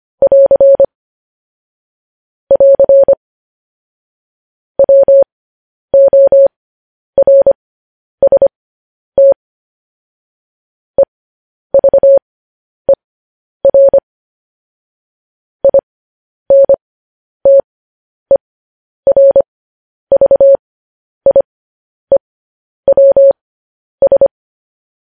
Morse code, heard as a keyed tone, about 25 wpm, spaced as for 10 wpm, 550 Hz. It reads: <AR> <AR> WORST EVER INTERVIEWS